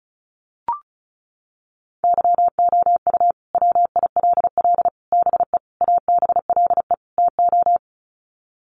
Morse code, heard as a keyed tone, about 34 wpm, 700 Hz. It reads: YOU WILL BE ABLE TO
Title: Sets of 5 Words Encore2 35wpm